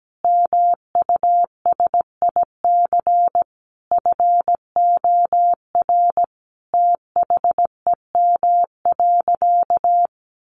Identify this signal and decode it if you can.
Morse code, heard as a keyed tone, about 17 wpm, 700 Hz. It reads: MUSIC FOR THEM.
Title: Sentences from Top 300 Words 17wpm